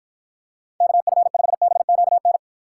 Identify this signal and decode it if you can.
Morse code, heard as a keyed tone, about 53 wpm, 700 Hz. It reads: KF5BQN